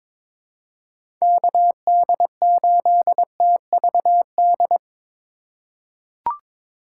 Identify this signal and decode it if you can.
Morse code, heard as a keyed tone, about 22 wpm, 700 Hz. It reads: KD8TVD
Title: USA Call Signs - 6 Characters - 4x Character Spacing - 22wpm